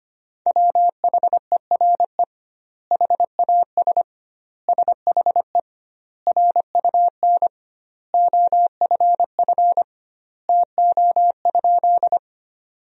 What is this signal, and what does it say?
Morse code, heard as a keyed tone, about 25 wpm, 700 Hz. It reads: WHERE HAS SHE RUN OFF TO?
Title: 2x Word-Spacing - Sentences from Top 300 Words - 25wpm